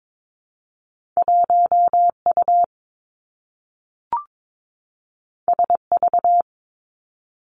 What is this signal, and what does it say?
Morse code, heard as a keyed tone, about 22 wpm, 700 Hz. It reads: 1 U S V
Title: Two Challenging Letter-Number Combinations 22wpm